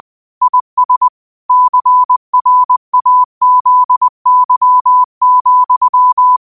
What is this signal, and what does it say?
Morse code, heard as a keyed tone, about 20 wpm, 1000 Hz. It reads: IS CRAZY,